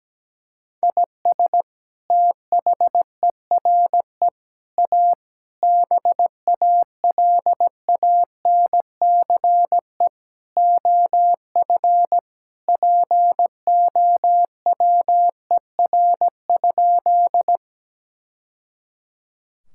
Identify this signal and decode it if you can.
Morse code, heard as a keyed tone, about 17 wpm, 700 Hz. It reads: IS THERE A BALANCE OF POWER?
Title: Sentences from Top 1000 Words - Review Part 6 - 17wpm